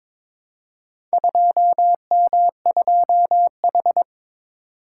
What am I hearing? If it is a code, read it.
Morse code, heard as a keyed tone, about 23 wpm, 700 Hz. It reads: 2M2H
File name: practice / 22wpm / Four Letter-Number Combinations - 22wpm